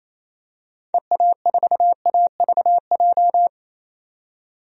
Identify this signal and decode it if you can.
Morse code, heard as a keyed tone, about 27 wpm, 700 Hz. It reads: EA4AVJ
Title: DXCC-100 Call Signs - 6 Characters - 28wpm